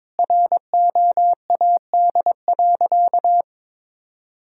Morse code, heard as a keyed tone, about 23 wpm, 700 Hz. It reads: ROAD.
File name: practice / 22wpm / Sentences with 2-Word Phrases - 22wpm